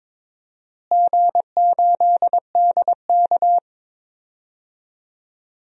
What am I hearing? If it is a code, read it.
Morse code, heard as a keyed tone, about 22 wpm, 700 Hz. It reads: G8DK
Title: DXCC-100 Call Signs - 4 Characters - Repeated 3x - 22wpm